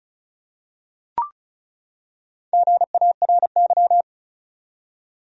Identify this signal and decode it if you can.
Morse code, heard as a keyed tone, about 34 wpm, 700 Hz. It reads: GARY
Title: USA Common Names - Max 4 Letters - 35wpm